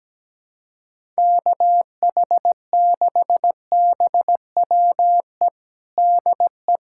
Morse code, heard as a keyed tone, about 17 wpm, 700 Hz. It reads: KH6BWE DE
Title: USA Callsign DE Callsign 17wpm